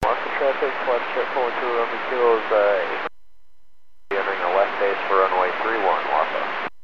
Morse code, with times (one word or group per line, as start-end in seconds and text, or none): none